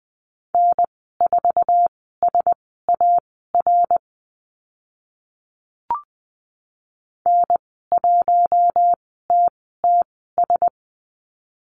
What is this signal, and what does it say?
Morse code, heard as a keyed tone, about 20 wpm, spaced as for 14 wpm, 700 Hz. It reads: N4SAR N1TTS
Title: 055 Lesson 10g - Letter r - Call Signs 20wpm